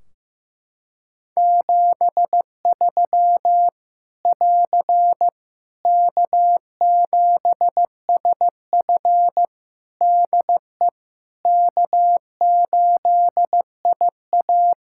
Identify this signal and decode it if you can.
Morse code, heard as a keyed tone, about 15 wpm, 700 Hz. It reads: 73 <AR> K7SF DE K8IA
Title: Practice QSO - Exchange 4 - Ending - 15wpm